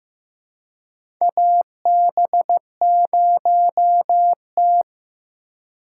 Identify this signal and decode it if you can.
Morse code, heard as a keyed tone, about 16 wpm, 700 Hz. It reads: AB0T
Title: USA Call Signs - 4x Character Spacing - 15wpm